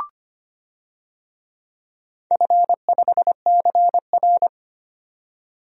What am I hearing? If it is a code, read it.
Morse code, heard as a keyed tone, about 25 wpm, 700 Hz. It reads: F5CR
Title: DXCC-100 Call Signs - 4 Characters - 25wpm